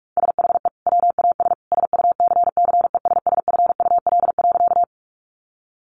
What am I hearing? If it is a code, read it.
Morse code, heard as a keyed tone, about 44 wpm, 700 Hz. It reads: SHE WAS SUCCESSFUL.
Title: Sentences from Top 900 Words 45wpm